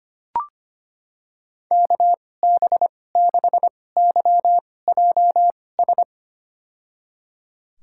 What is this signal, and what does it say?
Morse code, heard as a keyed tone, about 25 wpm, spaced as for 18 wpm, 700 Hz. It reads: KB6YJS
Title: USA Call Signs - 6 Characters - 2x Character Spacing - 25wpm